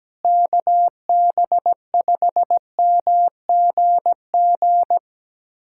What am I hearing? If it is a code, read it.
Morse code, heard as a keyed tone, about 17 wpm, 700 Hz. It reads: KB5MGG
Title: USA Callsign DE Callsign 17wpm